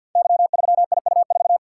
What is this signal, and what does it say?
Morse code, heard as a keyed tone, about 47 wpm, 700 Hz. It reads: Y3IU4
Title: Code Groups - 5 Letter:Numbers - 50wpm